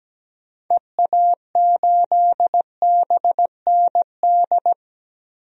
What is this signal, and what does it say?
Morse code, heard as a keyed tone, about 17 wpm, 700 Hz. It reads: EA8BND
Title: DXCC-100 Call Signs - 6 Characters - Repeated 3x - 17wpm